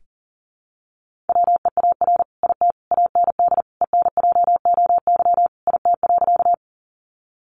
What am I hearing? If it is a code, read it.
Morse code, heard as a keyed tone, about 40 wpm, 700 Hz. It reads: WEAR IT AND ENJOY IT.
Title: Sentences from Top 1500 Words 40wpm